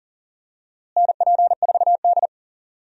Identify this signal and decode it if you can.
Morse code, heard as a keyed tone, about 37 wpm, 700 Hz. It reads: NP4D
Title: USA Call Signs - 4 Characters - Repeated 3x - 40wpm